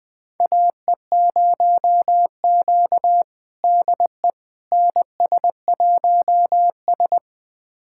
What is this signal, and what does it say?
Morse code, heard as a keyed tone, about 20 wpm, 700 Hz. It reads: AE0Q DE NS1S
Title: USA Callsign DE Callsign - 4 Characters - 20wpm